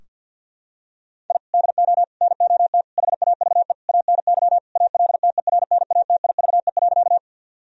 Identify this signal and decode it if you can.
Morse code, heard as a keyed tone, about 51 wpm, 700 Hz. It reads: I DO NOT HAVE ANY ALTERNATIVE.
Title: Sentences from Top 1000 Words - Review Part 4 - 50wpm